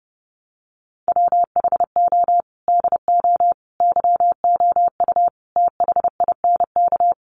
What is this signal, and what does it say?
Morse code, heard as a keyed tone, about 30 wpm, 700 Hz. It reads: WHO DO YOU THINK
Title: Sentences from Top 1100 Words 30wpm